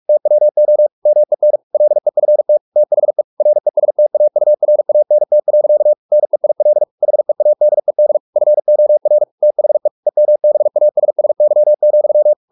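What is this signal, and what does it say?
Morse code, heard as a keyed tone, about 44 wpm, 600 Hz. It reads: TWO MEN LEFT THE RESTAURANT. NEIL HEADED FOR THE EMBASSY,
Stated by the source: Morse practice at 45 wpm, the novel Never by Ken Follett in English, from CW med Gnister.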